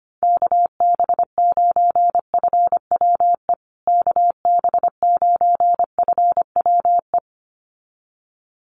Morse code, heard as a keyed tone, about 25 wpm, 700 Hz. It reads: KB9FWE KB9FWE
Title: USA Call Signs Repeated 2x 25wpm